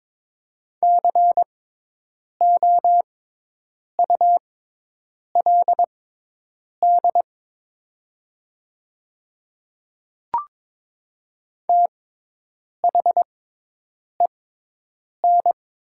Morse code, heard as a keyed tone, about 22 wpm, spaced as for 8 wpm, 700 Hz. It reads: COULD THEN
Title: 6x Character Spacing - Top 100 Words - 22wpm